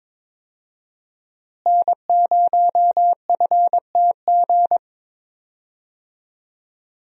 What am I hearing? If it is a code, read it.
Morse code, heard as a keyed tone, about 22 wpm, 700 Hz. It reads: N0FTG
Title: USA Call Signs - 5 Characters - Repeated 2x - 22wpm